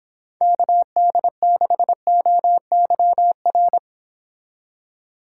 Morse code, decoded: KD6OYR

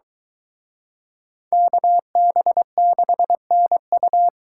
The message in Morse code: KB6NU

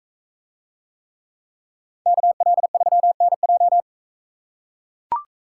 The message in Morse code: KL3NJ

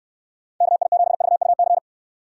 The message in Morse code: KI7VUC

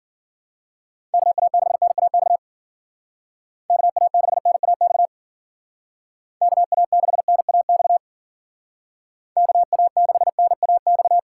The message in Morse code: KA6NAX KA6NAX KA6NAX KA6NAX

